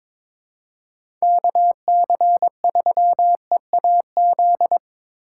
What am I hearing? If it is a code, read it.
Morse code, heard as a keyed tone, about 22 wpm, 700 Hz. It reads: KC3EAZ
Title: USA Call Signs - 6 Characters - 6x Character Spacing - 22wpm